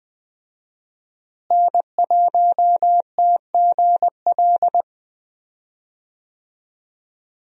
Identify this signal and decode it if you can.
Morse code, heard as a keyed tone, about 20 wpm, 700 Hz. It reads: N1TGL
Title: USA Call Signs - 5 Characters - Repeated 2x - 20wpm